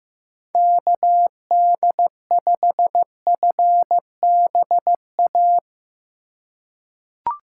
Morse code, heard as a keyed tone, about 15 wpm, 700 Hz. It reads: KD5FBA
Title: K1USN SST QSO Practice - 15wpm